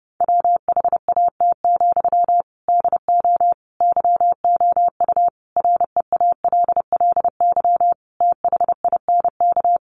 Morse code, heard as a keyed tone, about 30 wpm, 700 Hz. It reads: WHAT, DO YOU REALLY THINK
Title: Sentences from Top 600 Words 30wpm